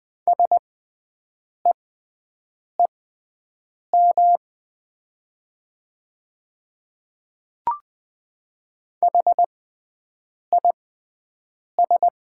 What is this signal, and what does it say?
Morse code, heard as a keyed tone, about 21 wpm, spaced as for 7 wpm, 700 Hz. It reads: SEEM HIS
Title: 6x Character Spacing - Top 300 Words - 20wpm